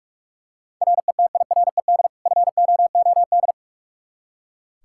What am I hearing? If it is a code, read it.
Morse code, heard as a keyed tone, about 46 wpm, 700 Hz. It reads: RETIRED FOOD
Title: Practice QSO - Exchange 3 - More Info - 45wpm